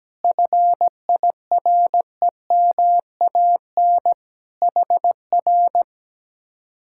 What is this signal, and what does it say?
Morse code, heard as a keyed tone, about 17 wpm, 700 Hz. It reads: FIREMAN HR
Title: Speed-Racing Common QSO Phrases - 17wpm